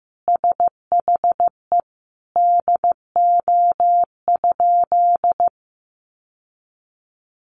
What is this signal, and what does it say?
Morse code, heard as a keyed tone, about 15 wpm, 700 Hz. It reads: SHE DO?
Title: Sentences from Top 800 Words 15wpm